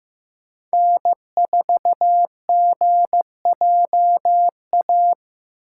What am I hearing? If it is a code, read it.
Morse code, heard as a keyed tone, about 15 wpm, 700 Hz. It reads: N4GJA
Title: USA Call Signs - 5 Characters - 4x Character Spacing - 15wpm